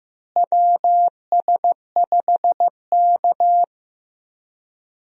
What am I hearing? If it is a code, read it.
Morse code, heard as a keyed tone, about 15 wpm, 700 Hz. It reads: WS5K
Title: USA Call Signs - 4 Characters - Repeated 2x - 15wpm